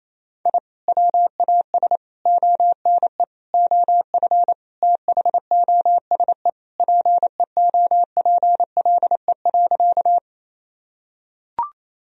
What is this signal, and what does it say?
Morse code, heard as a keyed tone, about 28 wpm, 700 Hz. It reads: I WAS ONE OF THOSE PEOPLE.
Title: Sentences from Top 1000 Words - Review Part 1 - 28wpm